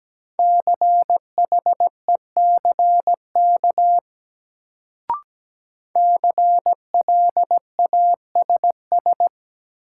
Morse code, heard as a keyed tone, about 17 wpm, 700 Hz. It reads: CHECK CLASS
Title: Sentences from Top 1400 Words - New Word Drill Alt - 17wpm